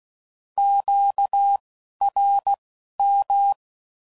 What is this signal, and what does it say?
Morse code, heard as a keyed tone, about 16 wpm, spaced as for 11 wpm, 800 Hz. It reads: QRM
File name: Mixed / Misc / Words